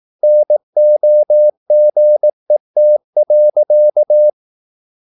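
Morse code, heard as a keyed tone, about 18 wpm, 600 Hz. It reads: NOGET.
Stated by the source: Slow (QRS) Morse practice at 18 wpm, CW med Gnister.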